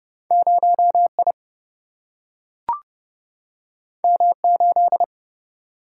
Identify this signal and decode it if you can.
Morse code, heard as a keyed tone, about 30 wpm, 700 Hz. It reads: 0 I M 8
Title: Single-Letter Single-Number 30wpm